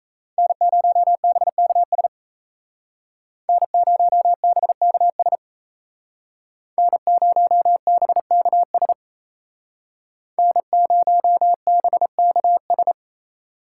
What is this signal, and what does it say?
Morse code, heard as a keyed tone, about 33 wpm, 700 Hz. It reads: N0BKS N0BKS N0BKS N0BKS
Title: Speed Racing - USA 1x3 Call Signs 28wpm